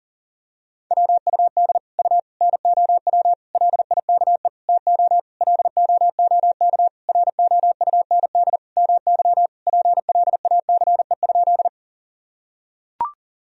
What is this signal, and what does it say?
Morse code, heard as a keyed tone, about 40 wpm, 700 Hz. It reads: WUD U NOW LIKE TO LOOK ROUND MY PLACE?
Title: Sentences from Top 200 Words and Abbreviations 40wpm